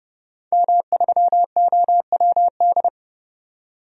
Morse code, heard as a keyed tone, about 30 wpm, 690 Hz. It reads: M3OWD
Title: DXCC-100 Call Signs - 5 Characters - 30wpm